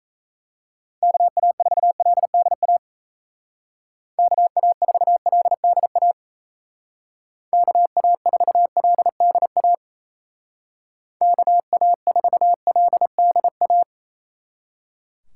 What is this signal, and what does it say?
Morse code, heard as a keyed tone, about 36 wpm, 700 Hz. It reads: KA4LDA KA4LDA KA4LDA KA4LDA